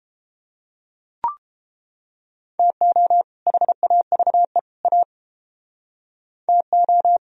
Morse code, heard as a keyed tone, about 29 wpm, 700 Hz. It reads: TO HAVE A TO